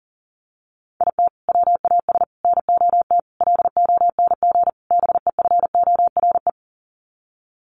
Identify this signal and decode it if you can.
Morse code, heard as a keyed tone, about 40 wpm, 700 Hz. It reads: IT WAS NOT LONG BEFORE
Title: Sets of 5 Words Encore2 40wpm